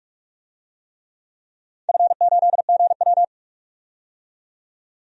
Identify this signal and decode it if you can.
Morse code, heard as a keyed tone, about 46 wpm, 700 Hz. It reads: F8GW
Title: DXCC-100 Call Signs - 4 Characters - 45wpm